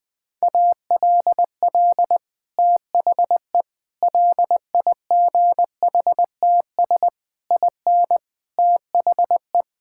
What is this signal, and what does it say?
Morse code, heard as a keyed tone, about 20 wpm, 700 Hz. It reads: ALL THE LIGHTS IN THE
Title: Sentences with 2-Word Phrases - 20wpm